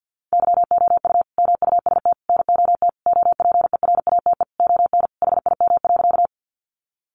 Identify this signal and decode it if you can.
Morse code, heard as a keyed tone, about 50 wpm, 700 Hz. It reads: YOU MUST NOT OPERATE ON HIM.
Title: Sentences from Top 1500 Words 50wpm